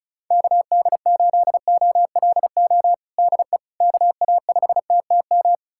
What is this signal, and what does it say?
Morse code, heard as a keyed tone, about 36 wpm, 700 Hz. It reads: KD8OLO DE KA5TTM
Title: USA Callsign DE Callsign 35wpm